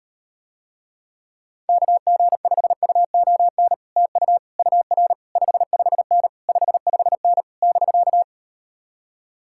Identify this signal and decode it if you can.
Morse code, heard as a keyed tone, about 39 wpm, 700 Hz. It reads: KG5UON TU UR 55N 55N <BK>